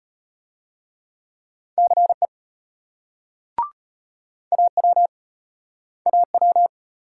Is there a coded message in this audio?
Yes: Morse code from 1.8 to 6.7 s, heard as a keyed tone, about 35 wpm, 700 Hz. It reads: CE AW AW